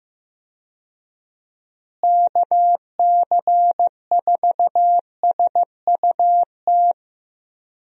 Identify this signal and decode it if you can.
Morse code, heard as a keyed tone, about 15 wpm, 700 Hz. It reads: KC4SUT
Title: USA Call Signs - 6 Characters - Repeated 2x - 15wpm